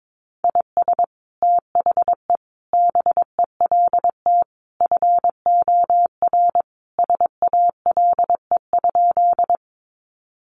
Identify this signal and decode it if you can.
Morse code, heard as a keyed tone, about 22 wpm, 700 Hz. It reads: IS THE BELT FOR SALE?
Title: Sentences from Top 1400 Words 22wpm